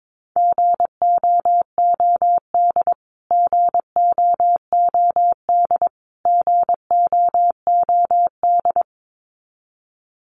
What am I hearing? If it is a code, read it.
Morse code, heard as a keyed tone, about 22 wpm, 700 Hz. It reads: GOOD GOOD GOOD
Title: Top 500 Words Repeated 22wpm